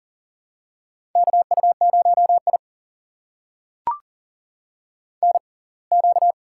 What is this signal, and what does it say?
Morse code, heard as a keyed tone, about 39 wpm, spaced as for 27 wpm, 700 Hz. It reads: KU0I NQ
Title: USA Call Signs - 4 Characters - 6x Character Spacing - 40wpm